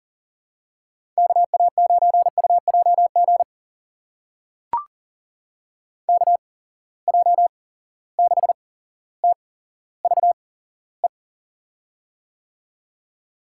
Morse code, heard as a keyed tone, about 40 wpm, spaced as for 17 wpm, 700 Hz. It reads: KA9UJG KJ6TVE
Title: USA Call Signs - 6 Characters - 8x Character Spacing - 40wpm